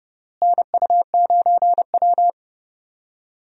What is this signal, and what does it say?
Morse code, heard as a keyed tone, about 31 wpm, 700 Hz. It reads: NU9W
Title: USA Call Signs - 4 Characters - Repeated 3x - 30wpm